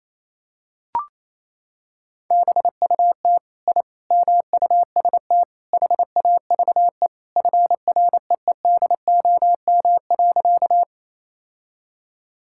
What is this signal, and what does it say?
Morse code, heard as a keyed tone, about 28 wpm, 700 Hz. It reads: BUT I MUST HAVE FREEDOM.
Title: Sentences from Top 1000 Words - Review Part 5 - 28wpm